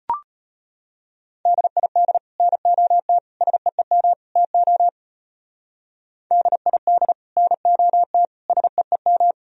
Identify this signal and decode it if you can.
Morse code, heard as a keyed tone, about 36 wpm, 700 Hz. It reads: DID NOT SEEM TO DID NOT SEEM